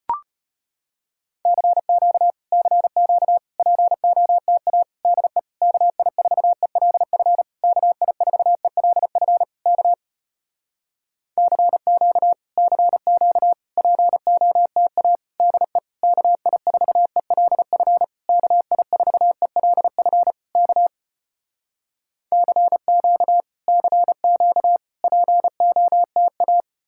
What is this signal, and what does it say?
Morse code, heard as a keyed tone, about 33 wpm, 700 Hz. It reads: CQ CQ POTA DE KI4ELF KI4ELF K CQ CQ POTA DE KI4ELF KI4ELF K CQ CQ POTA